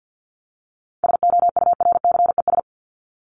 Morse code, heard as a keyed tone, about 48 wpm, 700 Hz. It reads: SOURCES
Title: ES:S Suffix - Top 100 Words with Base Words 50wpm